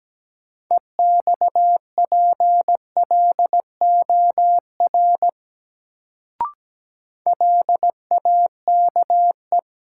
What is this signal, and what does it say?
Morse code, heard as a keyed tone, about 17 wpm, 700 Hz. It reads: EXPLOR LAKE
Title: Sentences from Top 1100 Words - New Word Drill Alt - 17wpm